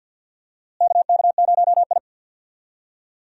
Morse code, heard as a keyed tone, about 48 wpm, 700 Hz. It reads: KK0I
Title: USA Call Signs - 4 Characters - Repeated 3x - 50wpm